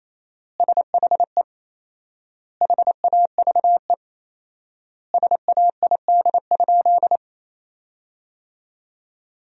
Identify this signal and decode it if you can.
Morse code, heard as a keyed tone, about 28 wpm, 700 Hz. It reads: SHE HAVE SAID?
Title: 4x Word-Spacing - Sentences from Top 100 Words - 28wpm